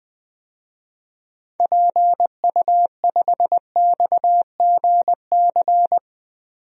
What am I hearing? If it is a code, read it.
Morse code, heard as a keyed tone, about 20 wpm, 700 Hz. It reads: PU5XGC
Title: DXCC-100 Call Signs - 6 Characters - 20wpm